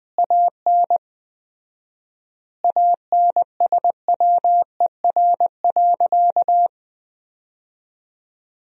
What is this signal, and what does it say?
Morse code, heard as a keyed tone, about 20 wpm, 700 Hz. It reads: AN ANSWER.
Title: 4x Word-Spacing - Sentences from Top 300 Words - 20wpm